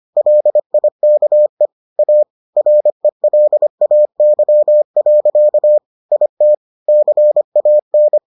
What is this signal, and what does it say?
Morse code, heard as a keyed tone, about 25 wpm, 600 Hz. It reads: LIKE A RELAY. IT CAN